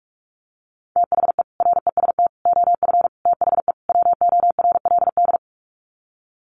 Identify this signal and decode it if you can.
Morse code, heard as a keyed tone, about 44 wpm, 700 Hz. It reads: THE REST OF THE WORLD
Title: Sets of 5 Words 45wpm